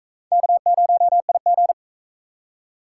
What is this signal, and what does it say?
Morse code, heard as a keyed tone, about 38 wpm, 700 Hz. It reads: K0IG